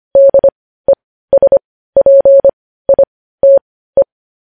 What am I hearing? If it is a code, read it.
Morse code, heard as a keyed tone, about 25 wpm, spaced as for 15 wpm, 550 Hz. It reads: DESPITE
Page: news headlines in morse code at 15 wpm